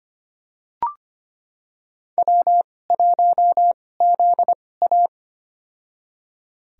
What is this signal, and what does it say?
Morse code, heard as a keyed tone, about 25 wpm, spaced as for 18 wpm, 700 Hz. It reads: W1ZA
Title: USA Call Signs - 2x Character Spacing - 25wpm